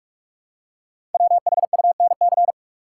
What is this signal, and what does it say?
Morse code, heard as a keyed tone, about 44 wpm, 700 Hz. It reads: WHUNC
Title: Code Groups - 5 Letters - 45wpm